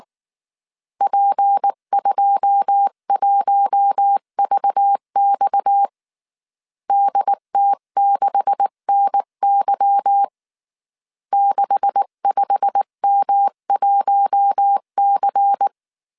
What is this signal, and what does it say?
Morse code, heard as a keyed tone, about 19 wpm, 800 Hz. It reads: P21VX DT6NY 65M1C
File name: CW / DET / 1 9 1500C S B 18wpm 2551